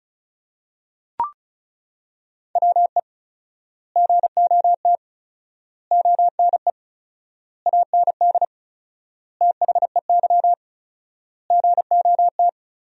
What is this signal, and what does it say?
Morse code, heard as a keyed tone, about 36 wpm, 700 Hz. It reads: WE GOT ONE AND THEY GOT
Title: 4x Word-Spacing - Sentences from Top 300 Words - 35wpm